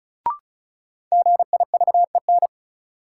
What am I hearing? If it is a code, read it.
Morse code, heard as a keyed tone, about 36 wpm, 700 Hz. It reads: GIVEN
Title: Sentences from Top 1400 Words - New Word Drill - 35wpm